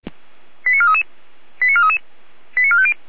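Five-tone selective call (ZVEI), heard as quick runs of five tones, tones of about 70 ms. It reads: 89433 89433 89450
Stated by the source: ZVEI five-tone call received by SDR.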